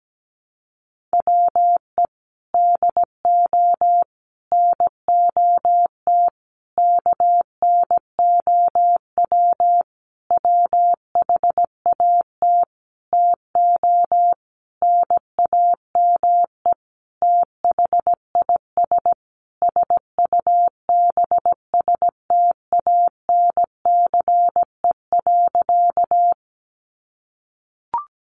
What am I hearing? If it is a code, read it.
Morse code, heard as a keyed tone, about 17 wpm, 700 Hz. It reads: WE DO NOT KNOW WHAT TO NAME THIS SUBSTANCE.
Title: Sentences from Top 1500 Words 17wpm